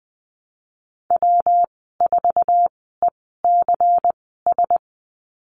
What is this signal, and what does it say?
Morse code, heard as a keyed tone, about 20 wpm, spaced as for 15 wpm, 700 Hz. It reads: W4ECS
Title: 186 Lesson 30g - Letter b - Call Signs 20wpm